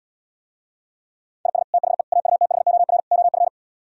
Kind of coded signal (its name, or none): Morse code